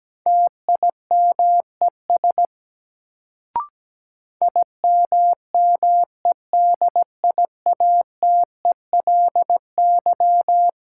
Morse code, heard as a keyed tone, about 17 wpm, 700 Hz. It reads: TIMES IMMEDIATELY
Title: Sentences from Top 1100 Words - New Word Drill Alt - 17wpm